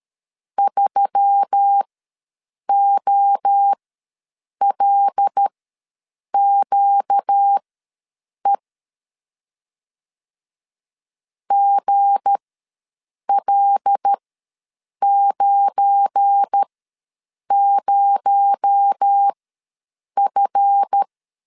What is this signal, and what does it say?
Morse code, heard as a keyed tone, about 13 wpm, spaced as for 7 wpm, 800 Hz. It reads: 3OLQE GL90F
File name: LEZIONI / 7 GF9Z / 7 GF9 ws 12wpm 3344